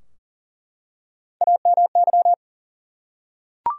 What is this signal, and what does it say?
Morse code, heard as a keyed tone, about 40 wpm, 700 Hz. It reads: AMY
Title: USA Common Names - Max 4 Letters - 40wpm